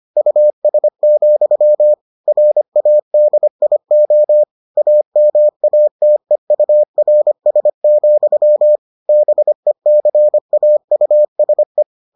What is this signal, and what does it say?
Morse code, heard as a keyed tone, about 25 wpm, 600 Hz. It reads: US, RADIO AMATEURS, BECAUSE